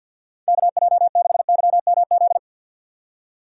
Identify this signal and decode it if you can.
Morse code, heard as a keyed tone, about 49 wpm, 700 Hz. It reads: KJ6YMZ
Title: USA Call Signs - 6 Characters - 6x Character Spacing - 50wpm